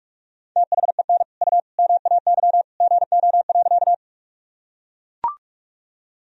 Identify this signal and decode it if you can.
Morse code, heard as a keyed tone, about 41 wpm, 700 Hz. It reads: THEN U MAY GO.
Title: Sentences from Top 100 Words and Abbreviations 45wpm